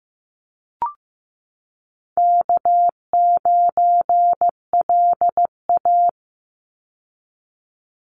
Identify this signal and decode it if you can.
Morse code, heard as a keyed tone, about 15 wpm, 700 Hz. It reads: K9LA
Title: Easy Call Signs - 15wpm